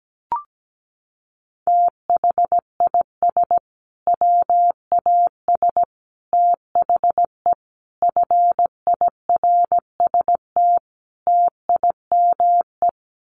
Morse code, heard as a keyed tone, about 17 wpm, 700 Hz. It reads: THIS WAS THE FIRST TIME
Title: Sets of 5 Words Encore 17wpm